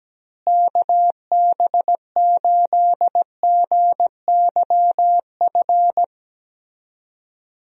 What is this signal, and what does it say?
Morse code, heard as a keyed tone, about 17 wpm, 700 Hz. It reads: KB8GYF
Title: USA Call Signs - 6 Characters - Repeated 3x - 17wpm